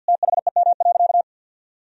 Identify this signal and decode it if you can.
Morse code, heard as a keyed tone, about 52 wpm, 700 Hz. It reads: THEM.
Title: Sentences from Top 1000 Words - Review Part 1 - 50wpm